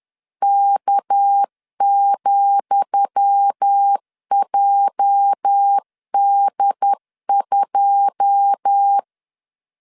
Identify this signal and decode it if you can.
Morse code, heard as a keyed tone, about 10 wpm, 800 Hz. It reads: K,JD2
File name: LEZIONI / 10 SIGNS / 10D COMA 10wpm 3013